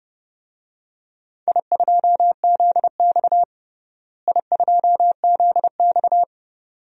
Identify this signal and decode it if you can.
Morse code, heard as a keyed tone, about 30 wpm, 700 Hz. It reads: I2ZX I2ZX
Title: DXCC-100 Call Signs - 4 Characters - Repeated 2x - 30wpm